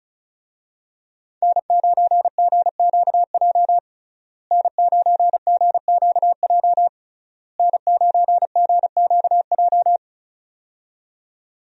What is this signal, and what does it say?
Morse code, heard as a keyed tone, about 34 wpm, 700 Hz. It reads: N9GQJ N9GQJ N9GQJ
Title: USA Call Signs - 5 Characters - Repeated 3x - 35wpm